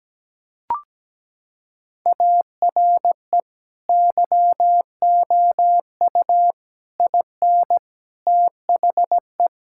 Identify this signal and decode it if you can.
Morse code, heard as a keyed tone, about 17 wpm, 700 Hz. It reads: ARE YOU IN THE
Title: Sentences from Top 1000 Words - Review Part 5 - 17wpm